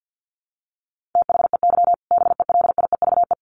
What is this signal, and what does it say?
Morse code, heard as a keyed tone, about 50 wpm, 700 Hz. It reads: THEY BELIEVE
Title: Sentences from Top 1000 Words 50wpm